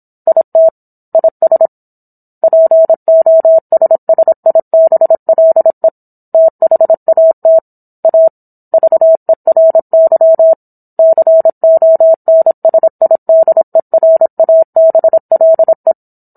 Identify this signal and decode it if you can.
Morse code, heard as a keyed tone, about 27 wpm, 650 Hz. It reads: IT IS POSSIBLE THAT A VERY CONSIDERABLE
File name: Book One / 17-Chapter 17 - 26 WPM